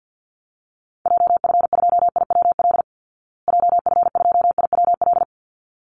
Morse code, heard as a keyed tone, about 48 wpm, 700 Hz. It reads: JF2IWL JF2IWL
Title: Call Signs Repeated 3x - 50wpm